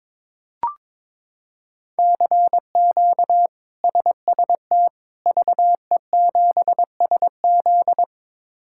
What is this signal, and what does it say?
Morse code, heard as a keyed tone, about 22 wpm, 700 Hz. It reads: CQ SST VE7SZ
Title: K1USN SST QSO Practice - 22wpm